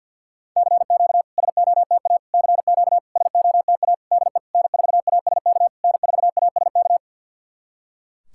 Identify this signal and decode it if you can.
Morse code, heard as a keyed tone, about 52 wpm, 690 Hz. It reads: CQ SOTA CQ SOTA DE N4ASK N4ASK
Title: SOTA QSO Practice 50wpm